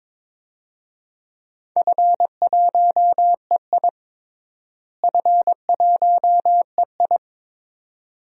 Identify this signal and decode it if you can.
Morse code, heard as a keyed tone, about 22 wpm, 700 Hz. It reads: F1EI F1EI
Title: DXCC-100 Call Signs - 4 Characters - Repeated 2x - 22wpm